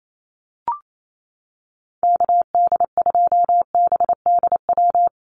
Morse code, heard as a keyed tone, about 28 wpm, 700 Hz. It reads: KD2BDW
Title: USA Call Signs Repeated 3x 28wpm